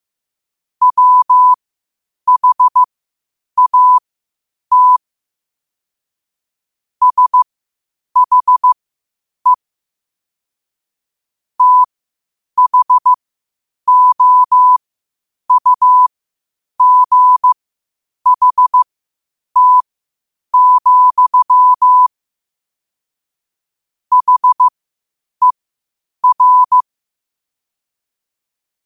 Morse code, heard as a keyed tone, about 15 wpm, spaced as for 8 wpm, 1000 Hz. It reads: WHAT SHE THOUGHT, HER
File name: qotdfiles / 8wpm